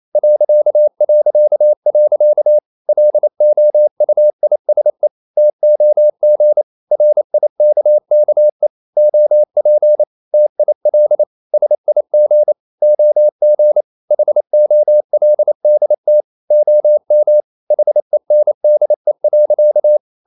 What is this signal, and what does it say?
Morse code, heard as a keyed tone, about 28 wpm, 600 Hz. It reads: ... LOUISE TOG RIKKE OP TIL SIG OG HOLDT OM HENDE.
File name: Ferie QRQ 28wpm